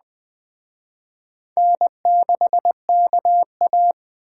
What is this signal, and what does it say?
Morse code, heard as a keyed tone, about 21 wpm, 700 Hz. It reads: N6KA